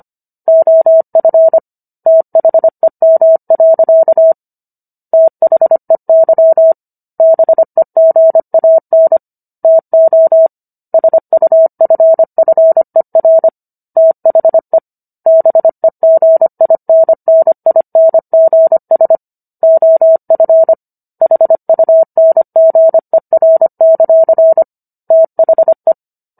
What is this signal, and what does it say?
Morse code, heard as a keyed tone, about 25 wpm, 650 Hz. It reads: OF THEM. THEY BEGAN TO SUFFER THE BEGINNINGS OF HUNGER; THE
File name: Book One / 16-Chapter 16 - 25 WPM